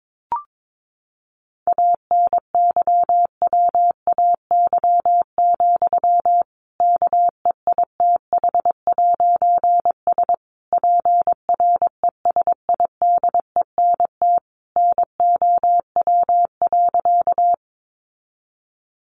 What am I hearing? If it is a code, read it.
Morse code, heard as a keyed tone, about 22 wpm, 700 Hz. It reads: ANYWAY, KEITH'S PRESIDENT NOW.
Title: A Sound Of Thunder 22wpm